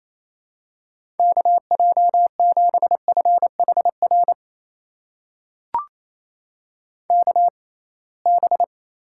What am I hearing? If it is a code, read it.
Morse code, heard as a keyed tone, about 28 wpm, spaced as for 21 wpm, 700 Hz. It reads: KJ7FHR KB
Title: USA Call Signs - 6 Characters - 6x Character Spacing - 28wpm